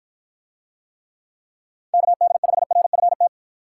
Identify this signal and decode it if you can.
Morse code, heard as a keyed tone, about 54 wpm, 700 Hz. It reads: KD5RFT